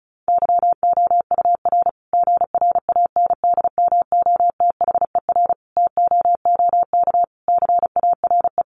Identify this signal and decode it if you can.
Morse code, heard as a keyed tone, about 34 wpm, 700 Hz. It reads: YOUR GRANDMOTHER TOOK CARE
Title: Sentences from Top 1200 Words 35wpm